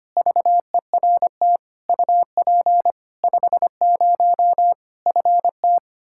Morse code, heard as a keyed tone, about 25 wpm, 700 Hz. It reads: VERT UP 50 FT
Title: Practice QSO - Exchange 2 - Info - 25wpm